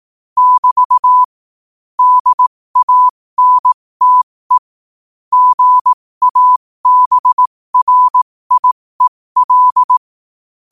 Morse code, heard as a keyed tone, about 18 wpm, spaced as for 15 wpm, 1000 Hz. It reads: <BT> DANTE GABRIEL
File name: qotdfiles / 15wpm